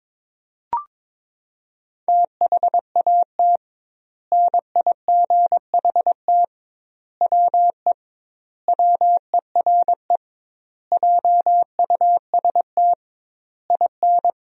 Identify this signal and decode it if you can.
Morse code, heard as a keyed tone, about 22 wpm, 700 Hz. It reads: THAT NIGHT WE WERE JUST IN
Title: 2x Word-Spacing - Sentences from Top 300 Words - 22wpm